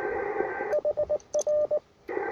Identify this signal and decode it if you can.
Morse code, heard as a keyed tone, about 19 wpm, 600 Hz, with radio noise. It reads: HR
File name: attachments / UR4MCK IZ5GVP